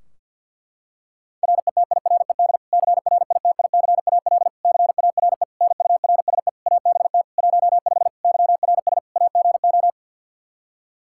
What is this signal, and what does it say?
Morse code, heard as a keyed tone, about 52 wpm, 700 Hz. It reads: RETIRED CRITICAL CARE NURSE ABT 15 YRS AGO
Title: Practice QSO - Exchange 3 - More Info - 50wpm